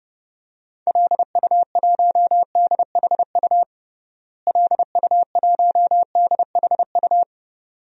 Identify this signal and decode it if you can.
Morse code, heard as a keyed tone, about 30 wpm, 700 Hz. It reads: LU1DHU LU1DHU
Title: DXCC-100 Call Signs - 6 Characters - Repeated 2x - 30wpm